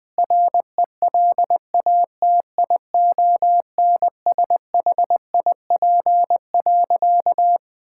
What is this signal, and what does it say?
Morse code, heard as a keyed tone, about 20 wpm, 700 Hz. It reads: RELATIONSHIP.
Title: Sentences with 6-Word Phrases - 20wpm